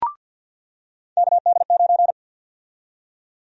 Morse code, heard as a keyed tone, about 53 wpm, 700 Hz. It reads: KD9
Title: USA Call Sign Prefixes - 50wpm